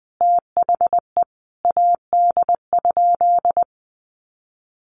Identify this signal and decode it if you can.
Morse code, heard as a keyed tone, about 20 wpm, 700 Hz. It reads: THE AD?
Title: Sentences from Top 1300 Words 20wpm